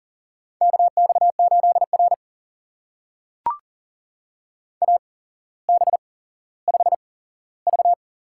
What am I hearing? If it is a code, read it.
Morse code, heard as a keyed tone, about 40 wpm, spaced as for 17 wpm, 700 Hz. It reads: KX8R AB5V
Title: USA Call Signs - 4 Characters - 8x Character Spacing - 40wpm